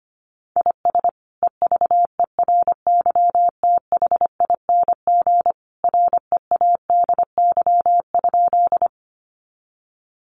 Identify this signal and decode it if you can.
Morse code, heard as a keyed tone, about 25 wpm, 700 Hz. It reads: IS EVERYTHING READY?
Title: Sentences from Top 700 Words 25wpm